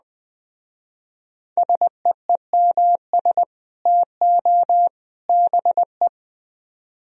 Morse code, heard as a keyed tone, about 20 wpm, 700 Hz. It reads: SEEMS TO BE